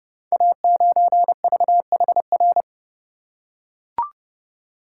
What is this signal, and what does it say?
Morse code, heard as a keyed tone, about 30 wpm, 700 Hz. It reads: A9VHR
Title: Code Groups - 5 Letter:Numbers - 30wpm